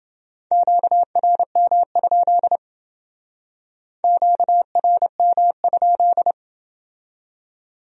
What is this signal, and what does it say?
Morse code, heard as a keyed tone, about 28 wpm, 700 Hz. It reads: QRM? QRM?